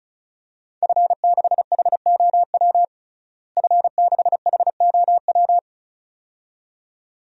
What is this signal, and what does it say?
Morse code, heard as a keyed tone, about 36 wpm, 700 Hz. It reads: F6HOW F6HOW
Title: DXCC-100 Call Signs - 5 Characters - Repeated 2x - 35wpm